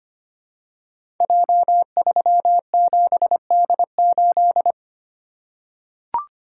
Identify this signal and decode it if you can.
Morse code, heard as a keyed tone, about 25 wpm, 700 Hz. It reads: J37D8
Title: Code Groups - 5 Letter:Numbers - 25wpm